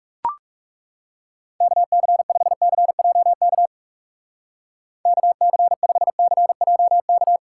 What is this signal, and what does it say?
Morse code, heard as a keyed tone, about 37 wpm, 700 Hz. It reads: KC5CJK KC5CJK
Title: Speed Racing - USA 2x3 Call Signs 30wpm